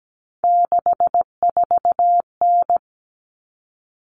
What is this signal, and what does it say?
Morse code, heard as a keyed tone, about 17 wpm, 700 Hz. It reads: 64N
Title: All Cut Numbers - 3-Digit 17wpm